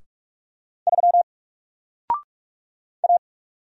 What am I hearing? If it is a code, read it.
Morse code, heard as a keyed tone, about 44 wpm, 700 Hz. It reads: 3 A
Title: Single Character (Letter:Number:Punctuation:Prosign) 45wpm